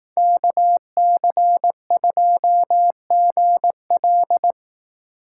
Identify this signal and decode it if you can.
Morse code, heard as a keyed tone, about 18 wpm, 700 Hz. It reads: KC2GL
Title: Speed Racing - USA Call Signs 15wpm